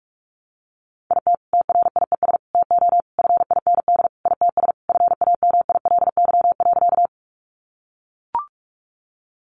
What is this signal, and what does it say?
Morse code, heard as a keyed tone, about 44 wpm, 700 Hz. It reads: IT TRIES TO FIND ITS FAMILY.
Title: Additional Easy Phrases 45wpm